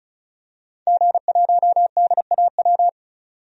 Morse code, heard as a keyed tone, about 36 wpm, 700 Hz. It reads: G1DAW
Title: DXCC-100 Call Signs - 5 Characters - 35wpm